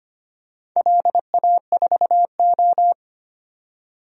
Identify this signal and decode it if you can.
Morse code, heard as a keyed tone, about 25 wpm, 700 Hz. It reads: LA4O
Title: DXCC-100 Call Signs - 4 Characters - 25wpm